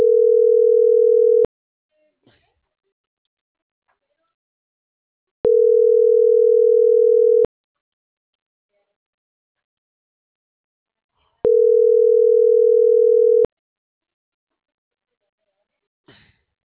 Ringback tone (North America), 440 + 480 Hz.